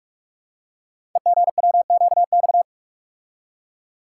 Morse code, heard as a keyed tone, about 48 wpm, 690 Hz. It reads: EGWQX
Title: Code Groups - 5 Letters - 45wpm